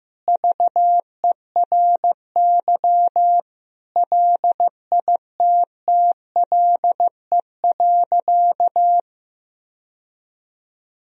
Morse code, heard as a keyed tone, about 16 wpm, 700 Hz. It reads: VERY LITTLE.
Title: Sentences from Top 1000 Words - Review Part 3 - 15wpm